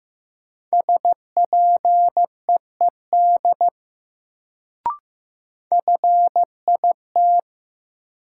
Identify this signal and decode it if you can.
Morse code, heard as a keyed tone, about 16 wpm, 700 Hz. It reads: SPEED FIT
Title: Sentences from Top 800 Words - New Word Drill - 15wpm